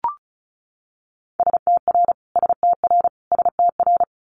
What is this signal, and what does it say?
Morse code, heard as a keyed tone, about 36 wpm, 700 Hz. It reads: STR STR STR
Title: Multi-letter Phonemes Repeated 3x 35wpm